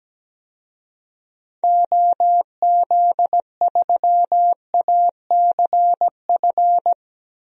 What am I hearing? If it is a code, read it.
Morse code, heard as a keyed tone, about 17 wpm, 700 Hz. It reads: OZ3ACF
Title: DXCC-100 Call Signs - 6 Characters - 17wpm